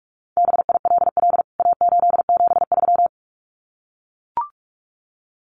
Morse code, heard as a keyed tone, about 44 wpm, 700 Hz. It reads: BILL A873
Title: CWT Exchange 45wpm